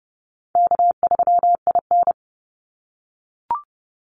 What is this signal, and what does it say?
Morse code, heard as a keyed tone, about 30 wpm, 700 Hz. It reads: K3IN
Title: Easy Call Signs - 30wpm